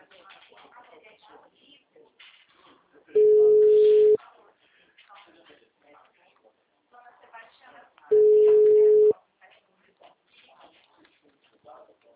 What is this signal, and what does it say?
Ringback tone (Europe, 425 Hz), with speech.